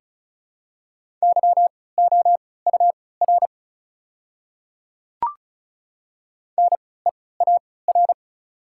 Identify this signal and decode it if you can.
Morse code, heard as a keyed tone, about 35 wpm, spaced as for 20 wpm, 700 Hz. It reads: YOUR NEAR
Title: 3x Character Spacing - Top 200 Words - 35wpm